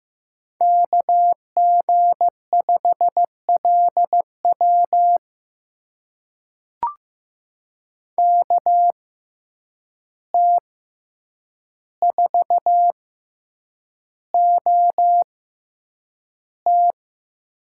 Morse code, heard as a keyed tone, about 15 wpm, spaced as for 8 wpm, 700 Hz. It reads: KG5LW KT4OT
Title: USA Call Signs - 5 Characters - 6x Character Spacing - 15wpm